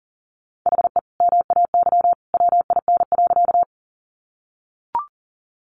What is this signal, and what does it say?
Morse code, heard as a keyed tone, about 40 wpm, 700 Hz. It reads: HE MAY WIN.
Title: Sentences from Top 1500 Words 40wpm